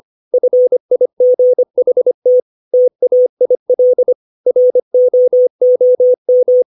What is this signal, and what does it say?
Morse code, heard as a keyed tone, about 25 wpm, 500 Hz. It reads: FIGHT TAIL ROOM
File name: 632 words / 632 Words 25WPM